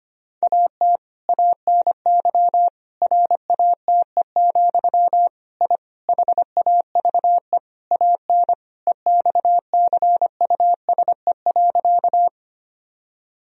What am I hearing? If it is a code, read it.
Morse code, heard as a keyed tone, about 25 wpm, 700 Hz. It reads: AT ANY RATE, I HAVE AN EXCUSE.
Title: Sentences from Top 1000 Words - Review Part 4 - 25wpm